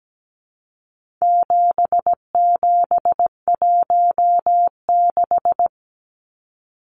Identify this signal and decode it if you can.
Morse code, heard as a keyed tone, about 17 wpm, 700 Hz. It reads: 7716
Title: Numbers - 4-Digit 17wpm